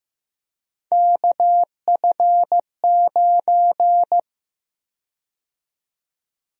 Morse code, heard as a keyed tone, about 15 wpm, 700 Hz. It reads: KF9
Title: USA Call Sign Prefixes - 15wpm